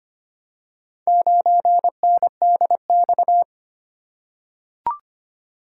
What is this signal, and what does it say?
Morse code, heard as a keyed tone, about 25 wpm, 700 Hz. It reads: 9NDX
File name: practice / 25wpm / Four Letter-Number Combinations - 25wpm